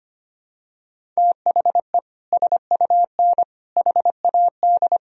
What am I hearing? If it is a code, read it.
Morse code, heard as a keyed tone, about 25 wpm, 700 Hz. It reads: THE SUN HAD
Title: Sentences from Top 1000 Words - Review Part 1 - 25wpm